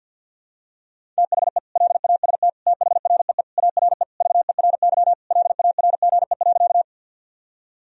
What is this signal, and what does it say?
Morse code, heard as a keyed tone, about 51 wpm, 700 Hz. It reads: THE LAST THREE ARE VERY LARGE.
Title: Sentences from Top 1000 Words - Review Part 1 - 50wpm